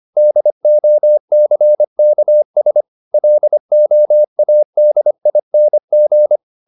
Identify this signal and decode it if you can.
Morse code, heard as a keyed tone, about 25 wpm, 600 Hz. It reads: DOCKS LOADING